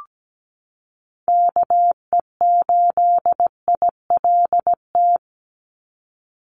Morse code, heard as a keyed tone, about 17 wpm, 700 Hz. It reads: KE8ILT
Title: USA Call Signs 17wpm